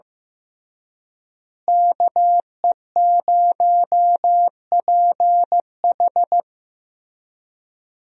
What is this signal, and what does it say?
Morse code, heard as a keyed tone, about 16 wpm, 700 Hz. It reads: KE0PH
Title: USA Call Signs - 5 Characters - Repeated 3x - 15wpm